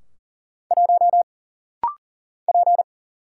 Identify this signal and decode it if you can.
Morse code, heard as a keyed tone, about 40 wpm, 700 Hz. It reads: 1 P
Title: Single Letter-Number Alt - Rapid-Fire 40wpm